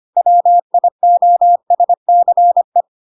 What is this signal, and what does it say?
Morse code, heard as a keyed tone, about 25 wpm, 700 Hz. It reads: WIOSCE